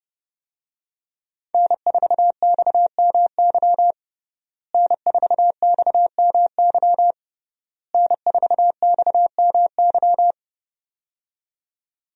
Morse code, heard as a keyed tone, about 31 wpm, 700 Hz. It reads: N4XMY N4XMY N4XMY
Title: USA Call Signs - 5 Characters - Repeated 3x - 30wpm